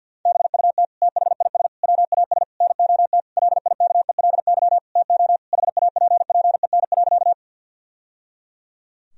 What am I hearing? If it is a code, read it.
Morse code, heard as a keyed tone, about 50 wpm, 700 Hz. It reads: BUT THIS WAS NOT LIKELY TO HAPPEN.
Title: Sentences from Top 1000 Words - Review Part 4 - 50wpm